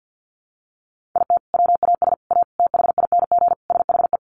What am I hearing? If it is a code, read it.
Morse code, heard as a keyed tone, about 50 wpm, 700 Hz. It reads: IT WAS A THING SHE
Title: Sentences from Top 1400 Words 50wpm